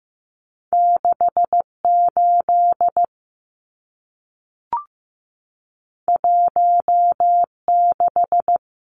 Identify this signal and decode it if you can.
Morse code, heard as a keyed tone, about 15 wpm, 700 Hz. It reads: 68 16
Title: Numbers - 2-Digit 15wpm